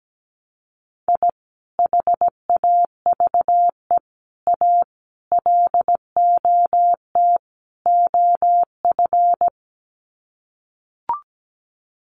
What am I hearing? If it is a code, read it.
Morse code, heard as a keyed tone, about 17 wpm, 700 Hz. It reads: I HAVE A LOT OF
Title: Sets of 5 Words 17wpm